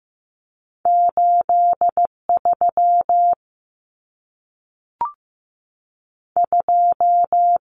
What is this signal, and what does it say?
Morse code, heard as a keyed tone, about 15 wpm, 700 Hz. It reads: 83 2
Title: Numbers - 2-Digit 15wpm